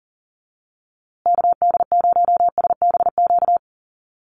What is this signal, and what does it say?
Morse code, heard as a keyed tone, about 40 wpm, 700 Hz. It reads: KD0SBQ
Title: USA Call Signs Repeated 3x 40wpm